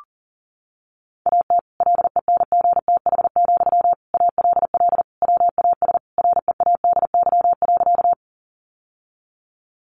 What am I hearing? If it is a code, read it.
Morse code, heard as a keyed tone, about 40 wpm, 700 Hz. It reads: AT LENGTH, ALL WAS READY.
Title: Sentences from Top 1000 Words 40wpm